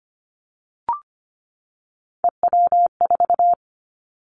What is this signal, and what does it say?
Morse code, heard as a keyed tone, about 25 wpm, 700 Hz. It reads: EW4
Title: Callsign Prefixes for Top 100 DXCC - 25wpm